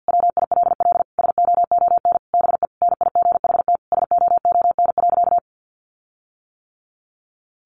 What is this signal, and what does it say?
Morse code, heard as a keyed tone, about 48 wpm, 700 Hz. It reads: WILL SOON BE NIGHT SOON.
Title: Sentences from Top 1300 Words 50wpm